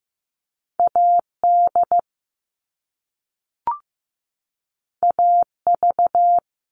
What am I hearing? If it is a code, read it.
Morse code, heard as a keyed tone, about 15 wpm, 700 Hz. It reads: AD AV
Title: All Cut Numbers - 2-Digit 15wpm